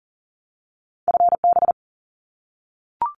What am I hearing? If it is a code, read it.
Morse code, heard as a keyed tone, about 40 wpm, 700 Hz. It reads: FB
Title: Abbreviations Repeated 3x - 40wpm